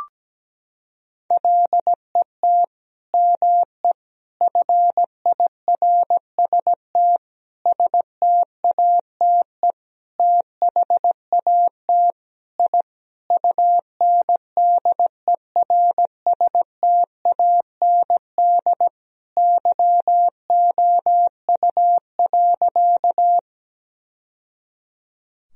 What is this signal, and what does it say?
Morse code, heard as a keyed tone, about 17 wpm, 700 Hz. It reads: LET ME FIRST STATE THAT I UNDERSTAND YOU.
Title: Sentences from Top 1000 Words - Review Part 2 - 17wpm